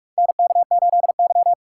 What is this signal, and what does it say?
Morse code, heard as a keyed tone, about 46 wpm, 690 Hz. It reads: NK8Y
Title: USA Call Signs - 4 Characters - 3x Character Spacing - 45wpm